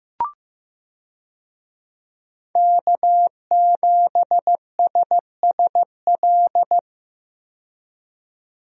Morse code, heard as a keyed tone, about 15 wpm, 700 Hz. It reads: K7SSL
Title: USA Call Signs - 5 Characters - 15wpm